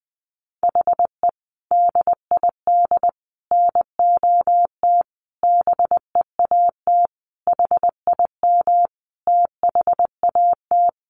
Morse code, heard as a keyed tone, about 20 wpm, 700 Hz. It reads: HE DID NOT BEAT HIM THAT
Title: Sentences from Top 1300 Words 20wpm